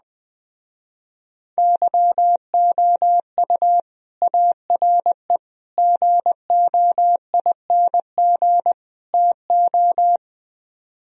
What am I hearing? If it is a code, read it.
Morse code, heard as a keyed tone, about 20 wpm, 700 Hz. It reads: YOU ARE GOING TO
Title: Speed-Racing - Sets of 4 Words 17wpm